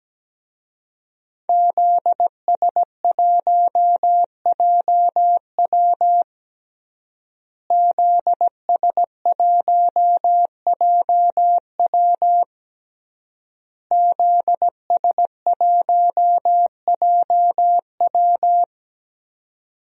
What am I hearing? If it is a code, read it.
Morse code, heard as a keyed tone, about 17 wpm, 700 Hz. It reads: ZS1JW ZS1JW ZS1JW
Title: DXCC-100 Call Signs - 5 Characters - Repeated 3x - 17wpm